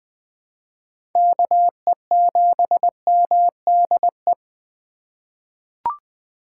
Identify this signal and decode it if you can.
Morse code, heard as a keyed tone, about 21 wpm, 700 Hz. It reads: KE7MDE
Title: USA Call Signs - 6 Characters - 2x Character Spacing - 20wpm